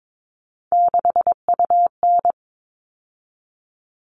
Morse code, heard as a keyed tone, about 22 wpm, 700 Hz. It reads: 6UN
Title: All Cut Numbers - 3-Digit 22wpm